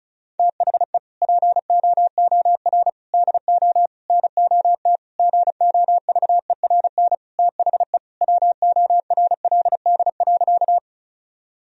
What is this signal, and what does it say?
Morse code, heard as a keyed tone, about 36 wpm, 700 Hz. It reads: THE POOR DO NOT GOVERN THE WORLD.
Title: Sentences from Top 1000 Words - Review Part 2 - 35wpm